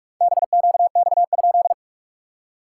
Morse code, heard as a keyed tone, about 46 wpm, 690 Hz. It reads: BQX?
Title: SOTA QSO Practice 45wpm